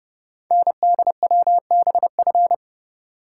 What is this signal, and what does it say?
Morse code, heard as a keyed tone, about 28 wpm, 690 Hz. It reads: NDWBF
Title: Code Groups - 5 Letters - 30wpm